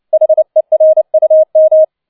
Morse code, heard as a keyed tone, about 29 wpm, 630 Hz. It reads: HERUM
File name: 007 Letzter Streich 30wpm